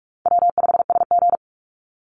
Morse code, heard as a keyed tone, about 44 wpm, 700 Hz. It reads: W5SG
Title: Easy Call Signs Repeated 2x - 45wpm